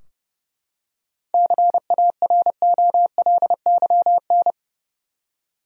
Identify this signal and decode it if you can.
Morse code, heard as a keyed tone, about 30 wpm, 700 Hz. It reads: CAROLYN
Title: USA Common Names - 30wpm